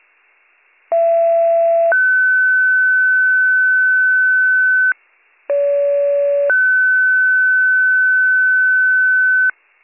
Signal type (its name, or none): two-tone page